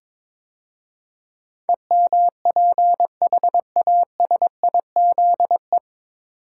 Morse code, heard as a keyed tone, about 22 wpm, 700 Hz. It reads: EMPHASIZE
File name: 9 Letter Words - First 3 Letters Given - 22wpm